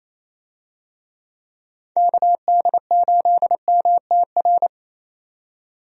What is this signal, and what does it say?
Morse code, heard as a keyed tone, about 27 wpm, 700 Hz. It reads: KD8MTR
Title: USA Call Signs - 6 Characters - 28wpm